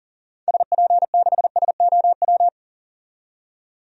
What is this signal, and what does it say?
Morse code, heard as a keyed tone, about 40 wpm, 700 Hz. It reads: SP6SOW
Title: DXCC-100 Call Signs - 6 Characters - 40wpm